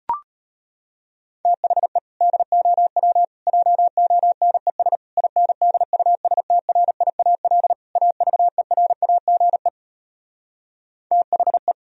Morse code, heard as a keyed tone, about 33 wpm, 700 Hz. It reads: THE DOW JONES INDUSTRIAL AVERAGE THE